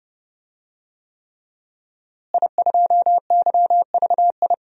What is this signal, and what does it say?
Morse code, heard as a keyed tone, about 32 wpm, 700 Hz. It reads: I2YVI
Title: DXCC-100 Call Signs - 5 Characters - Repeated 3x - 30wpm